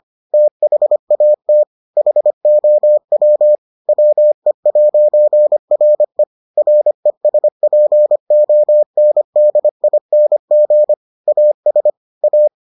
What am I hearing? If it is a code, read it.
Morse code, heard as a keyed tone, about 25 wpm, 600 Hz. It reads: THAT HOW WE'RE RESPONDING AS A